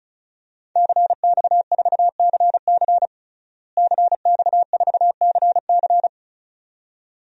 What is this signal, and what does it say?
Morse code, heard as a keyed tone, about 34 wpm, 700 Hz. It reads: CX4CC CX4CC
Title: DXCC-100 Call Signs - 5 Characters - Repeated 2x - 35wpm